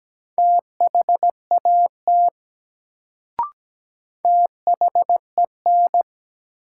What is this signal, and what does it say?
Morse code, heard as a keyed tone, about 17 wpm, 700 Hz. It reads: THAT THEN
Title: Sentences from Top 100 Words - New Word Drill - 17wpm